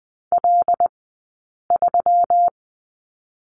Morse code, heard as a keyed tone, about 20 wpm, 700 Hz. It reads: L 3
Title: 120 Lesson 20d - Number 3 - Two Characters 20wpm